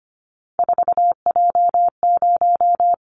Morse code, heard as a keyed tone, about 25 wpm, 700 Hz. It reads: 4J0
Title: Callsign Prefixes for Top 100 DXCC - 25wpm